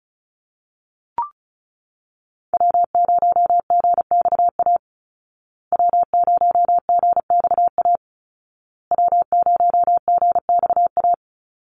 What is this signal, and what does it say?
Morse code, heard as a keyed tone, about 34 wpm, 700 Hz. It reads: W0GXA W0GXA W0GXA
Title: Call Signs Repeated 2x - 35wpm